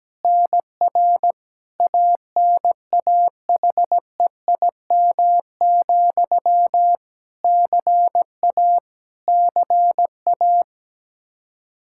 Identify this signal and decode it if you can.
Morse code, heard as a keyed tone, about 17 wpm, 700 Hz. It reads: NR ANAHEIM, CA CA
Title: Practice QSO - Exchange 1 - Beginning - 17wpm